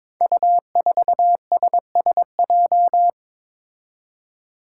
Morse code, heard as a keyed tone, about 23 wpm, 700 Hz. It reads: U4SSJ
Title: Five Challenging Letter-Number Combinations 22wpm